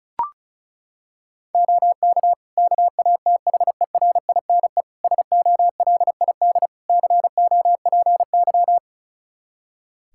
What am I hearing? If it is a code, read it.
Morse code, heard as a keyed tone, about 36 wpm, 700 Hz. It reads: OK KATHERINE SOLID COPY
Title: K1USN SST QSO Practice 35wpm